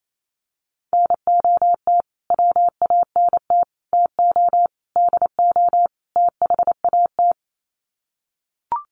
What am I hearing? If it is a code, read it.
Morse code, heard as a keyed tone, about 28 wpm, 700 Hz. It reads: NOT WANT TO DO THAT
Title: Sets of 5 Words Encore 28wpm